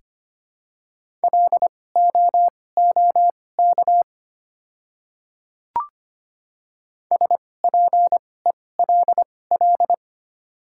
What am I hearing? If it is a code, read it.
Morse code, heard as a keyed tone, about 25 wpm, spaced as for 18 wpm, 700 Hz. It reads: LOOK SPELL
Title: 2x Character Spacing - Top 200 Words - 25wpm